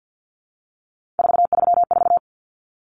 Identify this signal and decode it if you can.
Morse code, heard as a keyed tone, about 50 wpm, 700 Hz. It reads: 434
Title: Common Cut Numbers - 3-Digit 50wpm